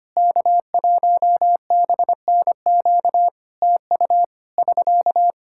Morse code, heard as a keyed tone, about 25 wpm, 700 Hz. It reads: K1BNQ TU <SK>
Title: Practice QSO - Exchange 4 - Ending - 25wpm